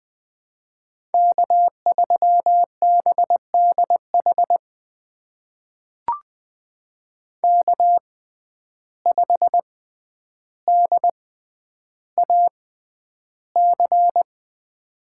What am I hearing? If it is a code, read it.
Morse code, heard as a keyed tone, about 20 wpm, spaced as for 10 wpm, 700 Hz. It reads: K3BDH K5DAC
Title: USA Call Signs - 5 Characters - 6x Character Spacing - 20wpm